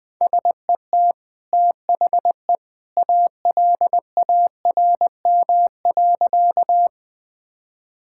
Morse code, heard as a keyed tone, about 21 wpm, 700 Hz. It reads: SET THE ALARM.
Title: Easy Sentences with Hints - 20wpm